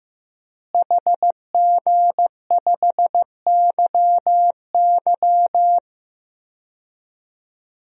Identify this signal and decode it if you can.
Morse code, heard as a keyed tone, about 16 wpm, 700 Hz. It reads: HG5YY
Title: DXCC-100 Call Signs - 5 Characters - Repeated 3x - 15wpm